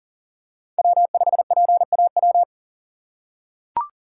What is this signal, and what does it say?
Morse code, heard as a keyed tone, about 40 wpm, 700 Hz. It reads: W5PAW
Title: USA Call Signs - 2x Character Spacing - 40wpm